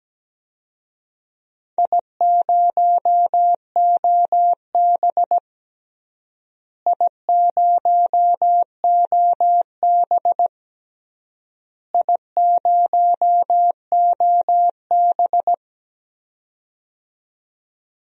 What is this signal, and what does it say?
Morse code, heard as a keyed tone, about 17 wpm, 700 Hz. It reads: I0OB I0OB I0OB
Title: DXCC-100 Call Signs - 4 Characters - Repeated 3x - 17wpm